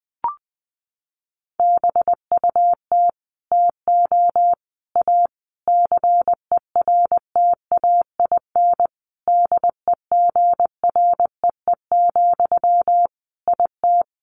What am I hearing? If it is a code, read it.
Morse code, heard as a keyed tone, about 20 wpm, 700 Hz. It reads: BUT TO A CERTAIN DEGREE, IT
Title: Sentences from Top 1500 Words 20wpm